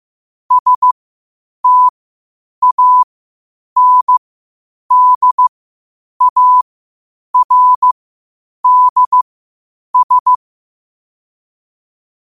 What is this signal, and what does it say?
Morse code, heard as a keyed tone, about 15 wpm, spaced as for 8 wpm, 1000 Hz. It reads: STANDARDS
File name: qotdfiles / 8wpm